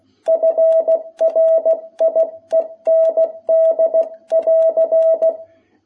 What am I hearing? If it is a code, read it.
Morse code, heard as a keyed tone, about 16 wpm, 660 Hz. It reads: FRIEND<AR>
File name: jh8cbh / 2021cw